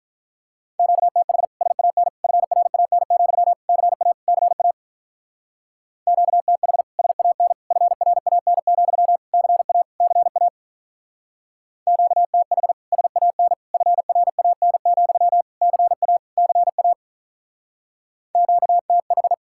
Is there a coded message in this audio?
Yes: Morse code from 0.8 to 19.3 s, heard as a keyed tone, about 44 wpm, 700 Hz. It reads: QTH SAN FRAN, CA CA QTH SAN FRAN, CA CA QTH SAN FRAN, CA CA QTH